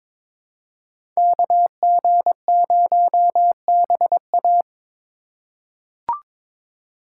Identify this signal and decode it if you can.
Morse code, heard as a keyed tone, about 22 wpm, 700 Hz. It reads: KG0BA
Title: USA Call Signs - 3x Character Spacing - 22wpm